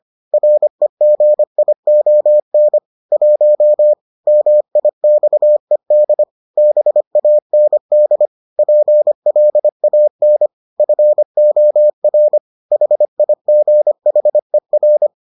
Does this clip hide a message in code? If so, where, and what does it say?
Yes: Morse code from 0.3 to 15.1 s, heard as a keyed tone, about 25 wpm, 600 Hz. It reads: REGION 1 MIXED BAND PLAN FOR HIGHER